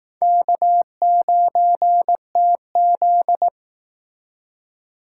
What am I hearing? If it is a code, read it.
Morse code, heard as a keyed tone, about 19 wpm, 700 Hz. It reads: K9TZ
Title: Speed Racing - USA 1x2 Call Signs 15wpm